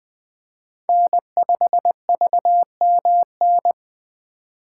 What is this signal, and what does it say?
Morse code, heard as a keyed tone, about 20 wpm, 700 Hz. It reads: N5VMN
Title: 2023 POTA Hunter Call Signs - 20wpm